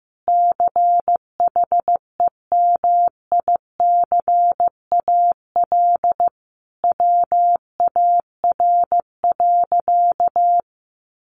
Morse code, heard as a keyed tone, about 15 wpm, 700 Hz. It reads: CHEMICAL WAR.
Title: Sentences from Top 1200 Words 15wpm